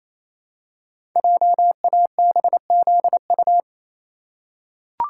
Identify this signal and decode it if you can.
Morse code, heard as a keyed tone, about 27 wpm, 700 Hz. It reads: JABZU
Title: Code Groups - 5 Letters - 28wpm